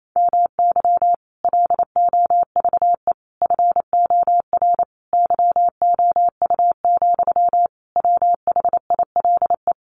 Morse code, heard as a keyed tone, about 28 wpm, 700 Hz. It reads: MY LOVE FOR YOU, WHILE
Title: Sentences from Top 1400 Words 28wpm